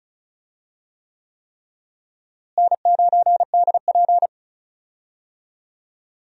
Morse code, heard as a keyed tone, about 36 wpm, 700 Hz. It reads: N9DP
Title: USA Call Signs - 4 Characters - Repeated 3x - 35wpm